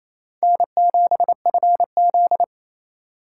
Morse code, heard as a keyed tone, about 35 wpm, 700 Hz. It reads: N7FZ
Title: Speed Racing - USA 1x2 Call Signs 28wpm